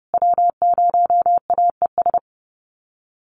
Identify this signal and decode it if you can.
Morse code, heard as a keyed tone, about 30 wpm, 700 Hz. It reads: W0AES
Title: USA Call Signs Repeated 3x 30wpm